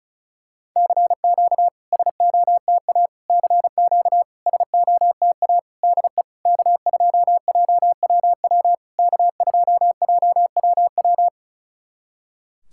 Morse code, heard as a keyed tone, about 34 wpm, 690 Hz. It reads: CQ SOTA CQ SOTA DE K2JWW K2JWW
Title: SOTA QSO Practice 35wpm